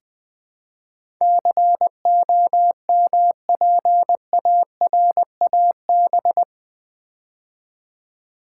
Morse code, heard as a keyed tone, about 20 wpm, 700 Hz. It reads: COMPARAB
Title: Build A Word - 4 Letters At A Time - 10-Letter Words - 20wpm